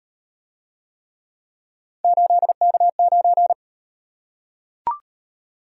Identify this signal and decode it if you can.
Morse code, heard as a keyed tone, about 38 wpm, 700 Hz. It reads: 8K9